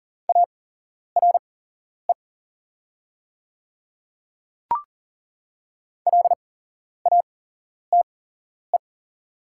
Morse code, heard as a keyed tone, about 40 wpm, spaced as for 11 wpm, 700 Hz. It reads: ARE LATE
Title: 8x Character Spacing - Top 300 Words - 40wpm